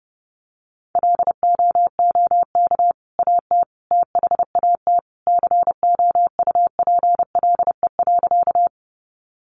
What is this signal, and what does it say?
Morse code, heard as a keyed tone, about 30 wpm, 700 Hz. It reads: LOOK AT THAT COUPLE.
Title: Sentences from Top 1100 Words 30wpm